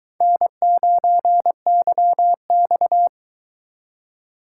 Morse code, heard as a keyed tone, about 24 wpm, 700 Hz. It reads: N9YX